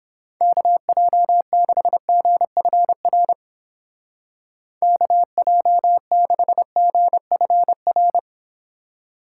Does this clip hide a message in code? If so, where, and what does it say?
Yes: Morse code from 0.4 to 8.2 s, heard as a keyed tone, about 29 wpm, 700 Hz. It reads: KJ6GFR KJ6GFR